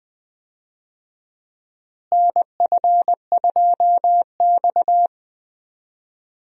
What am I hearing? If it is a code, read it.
Morse code, heard as a keyed tone, about 21 wpm, 700 Hz. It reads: NF2X
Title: USA Call Signs - 4 Characters - Repeated 2x - 20wpm